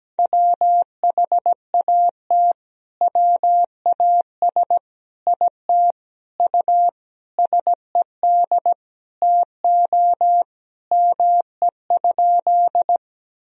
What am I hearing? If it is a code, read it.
Morse code, heard as a keyed tone, about 17 wpm, 700 Hz. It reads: WHAT WAS IT U SED TO ME?
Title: Sentences from Top 200 Words and Abbreviations 17wpm